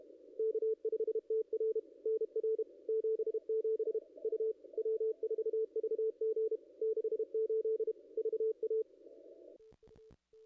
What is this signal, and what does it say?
Morse code, heard as a keyed tone, about 30 wpm, 440 Hz. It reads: K5TR NR 77 U W4VG 68 VA
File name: contests / ss-cw-k5tr-op-n5rz -0200